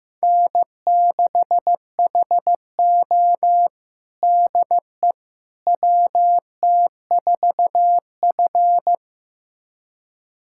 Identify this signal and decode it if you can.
Morse code, heard as a keyed tone, about 16 wpm, 700 Hz. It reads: N6HO DE WT4F
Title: USA Callsign DE Callsign - 4 Characters - 15wpm